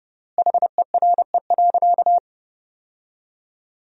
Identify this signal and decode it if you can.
Morse code, heard as a keyed tone, about 30 wpm, 700 Hz. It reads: HERE.
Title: 4x Word-Spacing - Sentences from Top 200 Words - 30wpm